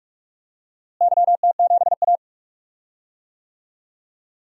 Morse code, heard as a keyed tone, about 45 wpm, 700 Hz. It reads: YT7A
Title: DXCC-100 Call Signs - 4 Characters - Repeated 3x - 45wpm